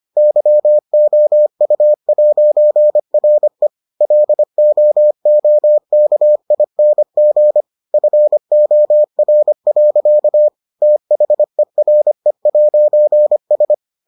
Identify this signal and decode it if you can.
Morse code, heard as a keyed tone, about 25 wpm, 600 Hz. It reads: YOU'RE LOOKING FOR. THERE'S